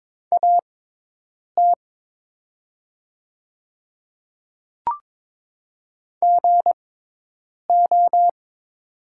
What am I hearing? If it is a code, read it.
Morse code, heard as a keyed tone, about 22 wpm, spaced as for 8 wpm, 700 Hz. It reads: AT GO
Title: 6x Character Spacing - Top 300 Words - 22wpm